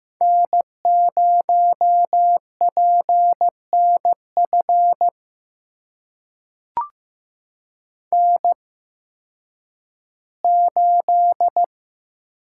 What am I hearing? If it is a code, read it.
Morse code, heard as a keyed tone, about 15 wpm, spaced as for 10 wpm, 700 Hz. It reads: N0PNF N8
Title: USA Call Signs - 5 Characters - 8x Character Spacing - 15wpm